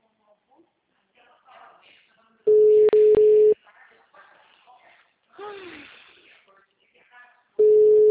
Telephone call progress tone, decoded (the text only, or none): ringback tone